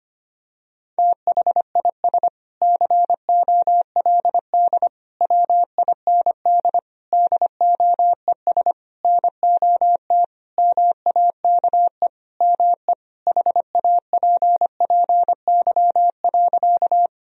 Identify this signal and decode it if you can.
Morse code, heard as a keyed tone, about 25 wpm, 700 Hz. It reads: THIS COLD WIND DOES NOT MAKE ME HAPPY.
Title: Sentences from Top 1000 Words - Review Part 2 - 25wpm